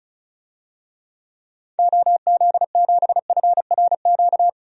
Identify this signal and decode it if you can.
Morse code, heard as a keyed tone, about 34 wpm, 700 Hz. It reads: OZ7FRQ
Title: DXCC-100 Call Signs - 6 Characters - Repeated 3x - 35wpm